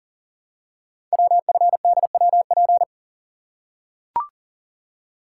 Morse code, heard as a keyed tone, about 42 wpm, 690 Hz. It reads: WFDWP
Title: Code Groups - 5 Letters - 40wpm